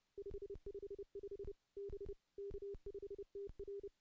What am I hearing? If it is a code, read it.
Morse code, heard as a keyed tone, about 31 wpm, 400 Hz. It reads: 555 B K5TR
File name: CW-SS-k5tr